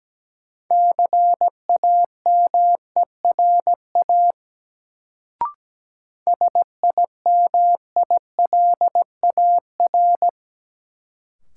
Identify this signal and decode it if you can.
Morse code, heard as a keyed tone, about 17 wpm, 700 Hz. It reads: CAMERA SIMILAR
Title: Sentences from Top 1000 Words - Review Part 5 - New Word Drill - 17wpm